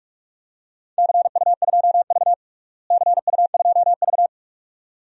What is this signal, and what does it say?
Morse code, heard as a keyed tone, about 47 wpm, 700 Hz. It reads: CU2V CU2V
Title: DXCC-100 Call Signs - 4 Characters - Repeated 2x - 45wpm